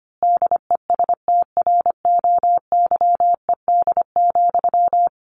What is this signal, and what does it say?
Morse code, heard as a keyed tone, about 25 wpm, 700 Hz. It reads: DESTROYED,
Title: A Sound Of Thunder 25wpm